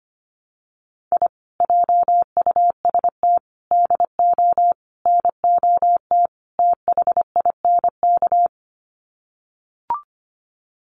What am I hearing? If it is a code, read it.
Morse code, heard as a keyed tone, about 25 wpm, 690 Hz. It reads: I JUST DO NOT THINK
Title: Sets of 5 Words Encore 25wpm